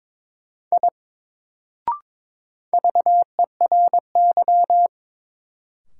Morse code, heard as a keyed tone, about 22 wpm, 700 Hz. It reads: I VERY
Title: Sentences from Top 200 Words - New Word Drill - 22wpm